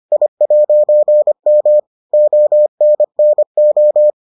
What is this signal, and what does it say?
Morse code, heard as a keyed tone, about 25 wpm, 600 Hz. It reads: I'M ONNO